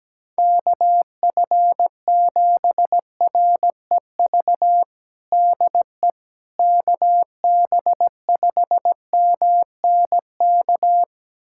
Morse code, heard as a keyed tone, about 17 wpm, 700 Hz. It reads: KF7REV DE KB5MNK
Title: USA Callsign DE Callsign - 6 Characters - 17wpm